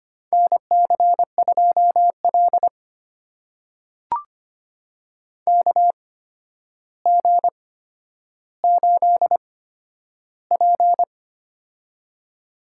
Morse code, heard as a keyed tone, about 25 wpm, spaced as for 11 wpm, 700 Hz. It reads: NC2L KG8P
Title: USA Call Signs - 4 Characters - 8x Character Spacing - 25wpm